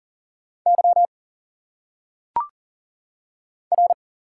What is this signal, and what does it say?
Morse code, heard as a keyed tone, about 39 wpm, 700 Hz. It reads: Y R
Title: Single Letters - Phonetic Pronunciation 40wpm